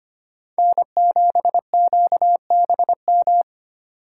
Morse code, heard as a keyed tone, about 25 wpm, 700 Hz. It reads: N7QBM
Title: USA Call Signs - 5 Characters - 3x Character Spacing - 25wpm